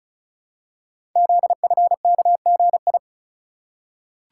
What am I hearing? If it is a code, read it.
Morse code, heard as a keyed tone, about 36 wpm, 700 Hz. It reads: ZFKGI
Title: Code Groups - 5 Letters - 35wpm